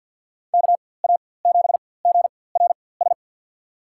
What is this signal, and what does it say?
Morse code, heard as a keyed tone, about 51 wpm, spaced as for 23 wpm, 700 Hz. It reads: KA7GRS
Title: USA Call Signs - 6 Characters - 4x Character Spacing - 50wpm